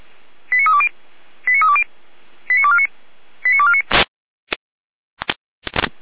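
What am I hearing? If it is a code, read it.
Five-tone selective call (ZVEI), heard as quick runs of five tones, tones of about 70 ms. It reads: 89320 89320 89249 89249
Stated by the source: Five-tone ZVEI selective call received by SDR.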